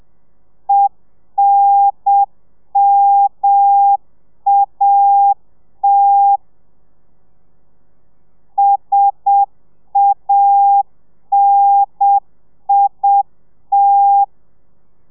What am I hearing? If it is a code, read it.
Morse code, heard as a keyed tone, about 7 wpm, 800 Hz. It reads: ENMAT SANIT